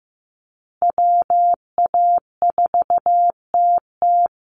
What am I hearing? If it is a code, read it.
Morse code, heard as a keyed tone, about 15 wpm, 700 Hz. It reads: WA4TT
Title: Call Signs Repeated 2x - 15wpm